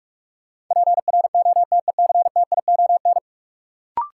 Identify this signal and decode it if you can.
Morse code, heard as a keyed tone, about 44 wpm, 700 Hz. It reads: PROTECTION
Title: Sentences from Top 1600 Words - New Word Drill Alt - 45wpm